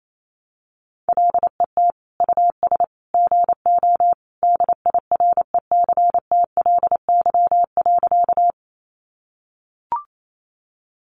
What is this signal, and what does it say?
Morse code, heard as a keyed tone, about 28 wpm, 700 Hz. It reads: LET US GO DIRECTLY.
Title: Sentences from Top 800 Words 28wpm